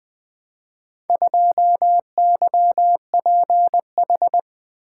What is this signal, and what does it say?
Morse code, heard as a keyed tone, about 20 wpm, 700 Hz. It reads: 2YPH
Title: Four Letter-Number Combinations - 20wpm